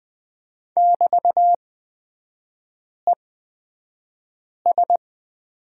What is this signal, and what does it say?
Morse code, heard as a keyed tone, about 20 wpm, spaced as for 5 wpm, 700 Hz. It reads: <BT>ES
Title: 224 Lesson 36e - BT - Three Characters 20wpm